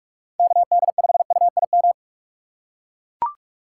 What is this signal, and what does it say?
Morse code, heard as a keyed tone, about 42 wpm, 700 Hz. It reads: KD5UIM
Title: USA Call Signs - 6x Character Spacing - 45wpm